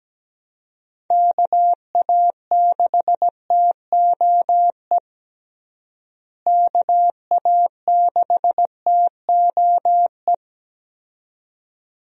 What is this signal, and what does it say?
Morse code, heard as a keyed tone, about 17 wpm, 700 Hz. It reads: KA6TOE KA6TOE
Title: USA Call Signs - 6 Characters - Repeated 2x - 17wpm